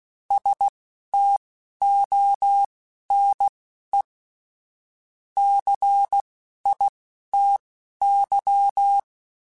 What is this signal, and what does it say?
Morse code, heard as a keyed tone, about 16 wpm, spaced as for 11 wpm, 800 Hz. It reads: STONE CITY